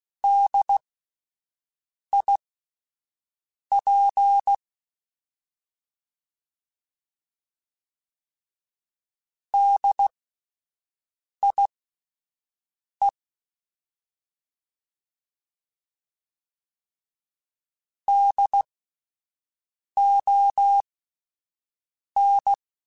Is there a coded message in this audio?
Yes: Morse code from 0.2 to 22.5 s, heard as a keyed tone, about 16 wpm, spaced as for 5 wpm, 800 Hz. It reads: DIP DIE DON